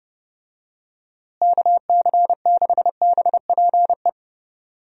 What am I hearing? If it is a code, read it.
Morse code, heard as a keyed tone, about 30 wpm, 700 Hz. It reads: KC6BPE
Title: USA Call Signs - 6 Characters - Repeated 2x - 30wpm